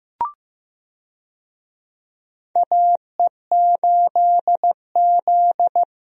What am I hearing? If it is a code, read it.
Morse code, heard as a keyed tone, about 15 wpm, 700 Hz. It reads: AE8Z
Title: USA Call Signs - 4 Characters - Repeated 2x - 15wpm